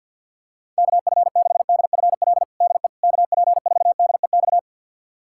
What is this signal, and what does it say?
Morse code, heard as a keyed tone, about 51 wpm, 700 Hz. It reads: KF6DFL DE KP4DEX
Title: USA Callsign DE Callsign - 6 Characters - 50wpm